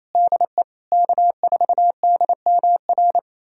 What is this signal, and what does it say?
Morse code, heard as a keyed tone, about 28 wpm, 700 Hz. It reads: DE K4DMR
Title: USA Callsign DE Callsign - 5 Characters - 28wpm